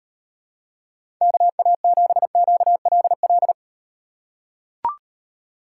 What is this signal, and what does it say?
Morse code, heard as a keyed tone, about 38 wpm, 700 Hz. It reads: KA7QLL